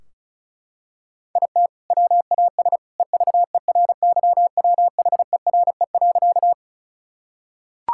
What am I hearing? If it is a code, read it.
Morse code, heard as a keyed tone, about 36 wpm, 690 Hz. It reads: IT WAS EVERYWHERE.
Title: Sentences from Top 1600 Words 35wpm